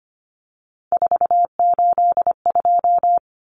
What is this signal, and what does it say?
Morse code, heard as a keyed tone, about 25 wpm, 700 Hz. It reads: 482
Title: Common Cut Numbers - 3-Digit 25wpm